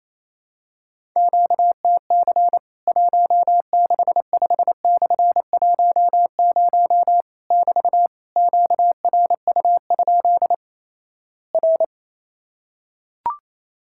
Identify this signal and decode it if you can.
Morse code, heard as a keyed tone, about 28 wpm, 700 Hz. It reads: QTC 165/10 <BT> QRU? R
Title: Worked All Europe DX Contest - QTC Exchange - 28wpm